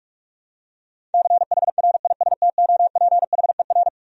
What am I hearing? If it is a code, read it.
Morse code, heard as a keyed tone, about 42 wpm, 700 Hz. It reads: CHRISTOPHER
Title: USA Common Names - 45wpm